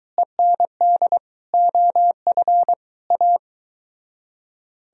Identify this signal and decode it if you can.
Morse code, heard as a keyed tone, about 20 wpm, 700 Hz. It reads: END OF A